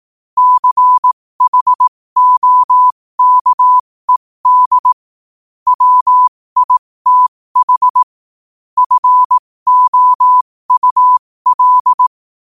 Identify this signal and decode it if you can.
Morse code, heard as a keyed tone, about 18 wpm, spaced as for 15 wpm, 1000 Hz. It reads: CHOKED WITH FOUL
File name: qotdfiles / 15wpm